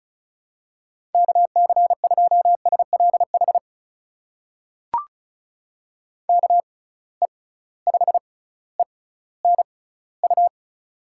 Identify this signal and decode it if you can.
Morse code, heard as a keyed tone, about 36 wpm, spaced as for 18 wpm, 700 Hz. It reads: KC2SLH KE5ENU
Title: USA Call Signs - 6 Characters - 6x Character Spacing - 35wpm